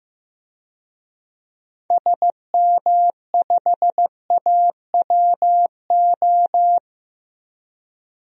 Morse code, heard as a keyed tone, about 15 wpm, 700 Hz. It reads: SM5AWO
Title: DXCC-100 Call Signs - 6 Characters - 15wpm